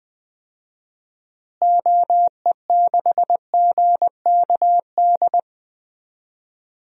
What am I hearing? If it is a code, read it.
Morse code, heard as a keyed tone, about 21 wpm, 700 Hz. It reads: OE6GKD
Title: DXCC-100 Call Signs - 6 Characters - Repeated 2x - 20wpm